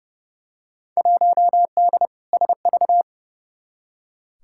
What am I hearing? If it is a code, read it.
Morse code, heard as a keyed tone, about 30 wpm, 700 Hz. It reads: 1D SV
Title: Field Day Exchanges - 30wpm